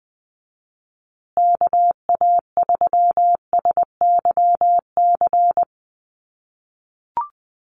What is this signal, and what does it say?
Morse code, heard as a keyed tone, about 20 wpm, 700 Hz. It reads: KA3SYC
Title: USA Call Signs Repeated 2x 20wpm